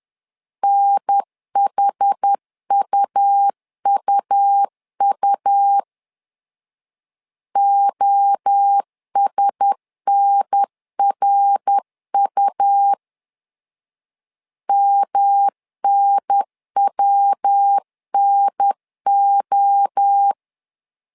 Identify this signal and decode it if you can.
Morse code, heard as a keyed tone, about 10 wpm, 800 Hz. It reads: NHUUU OSNRU MNWNO
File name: LEZIONI / 4 Q1S 10wpm 2133